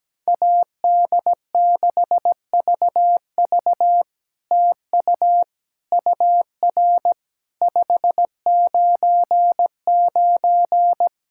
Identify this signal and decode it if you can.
Morse code, heard as a keyed tone, about 17 wpm, 700 Hz. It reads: AD6VV TU UR 599
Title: SOTA QSO Practice 17wpm